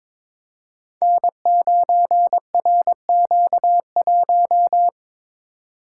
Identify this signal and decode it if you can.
Morse code, heard as a keyed tone, about 22 wpm, 700 Hz. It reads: N9RQ1
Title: Code Groups - 5 Letter:Numbers - 22wpm